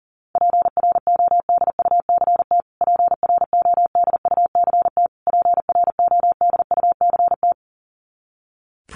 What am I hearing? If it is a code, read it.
Morse code, heard as a keyed tone, about 37 wpm, 700 Hz. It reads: PRODUCT PRODUCT PRODUCT
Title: Top 500 Words Repeated 40wpm